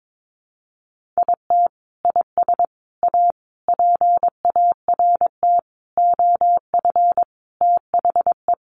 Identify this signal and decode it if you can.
Morse code, heard as a keyed tone, about 22 wpm, 700 Hz. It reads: IT IS A PART OF THE
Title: Sentences from Top 1400 Words 22wpm